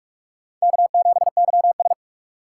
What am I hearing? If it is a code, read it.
Morse code, heard as a keyed tone, about 44 wpm, 700 Hz. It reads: K7YS
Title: USA Call Signs - 4 Characters - 4x Character Spacing - 45wpm